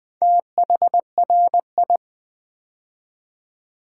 Morse code, heard as a keyed tone, about 20 wpm, 700 Hz. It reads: THRI
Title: Build A Word - 2 Letters At A Time - 6-Letter Words - 20wpm